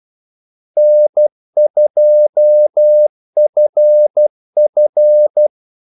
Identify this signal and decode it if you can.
Morse code, heard as a keyed tone, about 12 wpm, 600 Hz. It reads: N2FF
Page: 2 ekstra ordmellemrum mellem kaldesignalerne